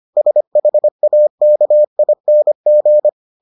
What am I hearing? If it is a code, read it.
Morse code, heard as a keyed tone, about 25 wpm, 600 Hz. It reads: SHAKING